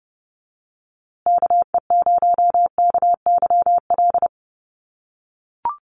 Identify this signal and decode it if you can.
Morse code, heard as a keyed tone, about 30 wpm, 700 Hz. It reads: KE0KYL
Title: USA Call Signs Repeated 3x 30wpm